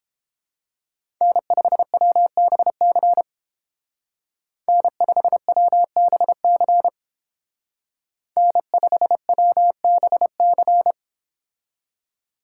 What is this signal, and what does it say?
Morse code, heard as a keyed tone, about 30 wpm, 700 Hz. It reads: N5WBC N5WBC N5WBC